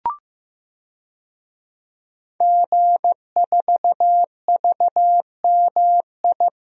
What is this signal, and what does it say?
Morse code, heard as a keyed tone, about 15 wpm, 700 Hz. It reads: G4VMI
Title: DXCC-100 Call Signs - 5 Characters - 15wpm